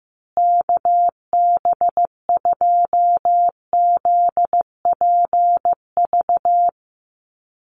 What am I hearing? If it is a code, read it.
Morse code, heard as a keyed tone, about 15 wpm, 700 Hz. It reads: KB2ZPV
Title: USA Call Signs Repeated 2x 15wpm